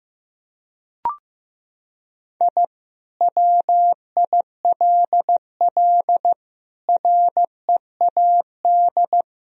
Morse code, heard as a keyed tone, about 15 wpm, 700 Hz. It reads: I WILL READ
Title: Sentences from Top 1000 Words - Review Part 4 - 15wpm